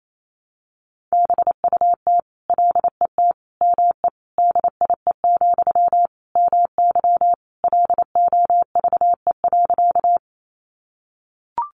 Morse code, heard as a keyed tone, about 28 wpm, 700 Hz. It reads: BUT LET ME DIE, MY LOVE.
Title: Sentences from Top 1300 Words 28wpm